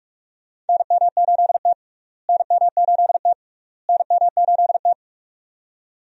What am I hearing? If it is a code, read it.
Morse code, heard as a keyed tone, about 42 wpm, 700 Hz. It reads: NM8T NM8T NM8T
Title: USA Call Signs - 4 Characters - Repeated 3x - 45wpm